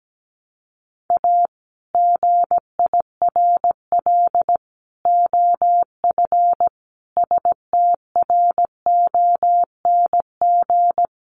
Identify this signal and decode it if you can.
Morse code, heard as a keyed tone, about 17 wpm, 700 Hz. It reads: A GIRL OF STRONG
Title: Sentences from Top 1200 Words 17wpm